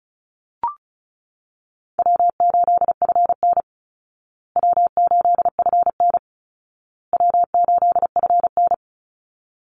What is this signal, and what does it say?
Morse code, heard as a keyed tone, about 34 wpm, 700 Hz. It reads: W8FN W8FN W8FN
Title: Call Signs Repeated 2x - 35wpm